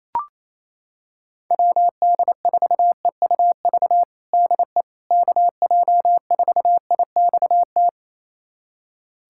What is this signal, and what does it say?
Morse code, heard as a keyed tone, about 28 wpm, 700 Hz. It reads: WD4EUV DE KJ4IXT
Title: USA Callsign DE Callsign - 6 Characters - 28wpm